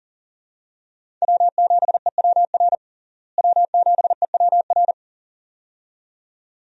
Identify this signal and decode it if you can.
Morse code, heard as a keyed tone, about 37 wpm, 700 Hz. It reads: W7EWR W7EWR
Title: USA Call Signs - 5 Characters - Repeated 2x - 40wpm